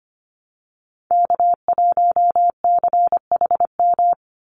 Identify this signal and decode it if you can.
Morse code, heard as a keyed tone, about 25 wpm, 700 Hz. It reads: K1CHM
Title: Call Signs Repeated 2x - 25wpm